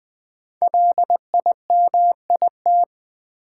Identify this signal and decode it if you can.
Morse code, heard as a keyed tone, about 20 wpm, 700 Hz. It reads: LIMIT
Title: Sentences from Top 1500 Words - New Word Drill Alt - 20wpm